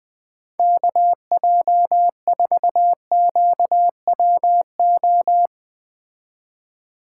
Morse code, heard as a keyed tone, about 19 wpm, 700 Hz. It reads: KJ4QWO
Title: Speed Racing - USA Call Signs 17wpm